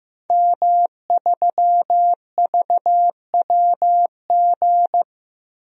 Morse code, heard as a keyed tone, about 15 wpm, 700 Hz. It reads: M3VWG
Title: DXCC-100 Call Signs - 5 Characters - 15wpm